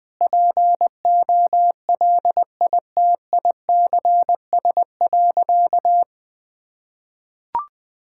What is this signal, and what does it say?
Morse code, heard as a keyed tone, about 20 wpm, 700 Hz. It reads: POLITICS.
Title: Sentences from Top 1000 Words - Review Part 6 - 20wpm